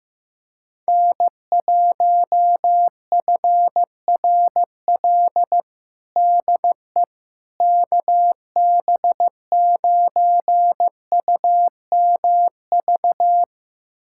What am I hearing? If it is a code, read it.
Morse code, heard as a keyed tone, about 16 wpm, 700 Hz. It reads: N1FRL DE KB9UMV
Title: USA Callsign DE Callsign 15wpm